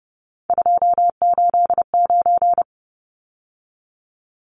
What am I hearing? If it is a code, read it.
Morse code, heard as a keyed tone, about 30 wpm, 700 Hz. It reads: 289
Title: Numbers - 3-Digit 30wpm